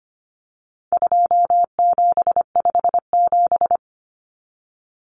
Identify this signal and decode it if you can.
Morse code, heard as a keyed tone, about 25 wpm, 700 Hz. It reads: 2757
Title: Common Cut Numbers - 4-Digit 25wpm